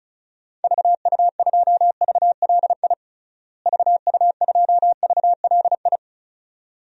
Morse code, heard as a keyed tone, about 34 wpm, 700 Hz. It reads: VU2VLI VU2VLI
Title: DXCC-100 Call Signs - 6 Characters - Repeated 2x - 35wpm